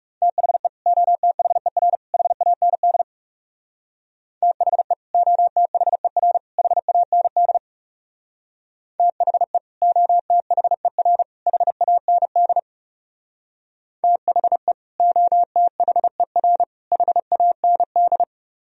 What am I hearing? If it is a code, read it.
Morse code, heard as a keyed tone, about 38 wpm, 690 Hz. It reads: THE OTHER HAND THE OTHER HAND THE OTHER HAND THE OTHER HAND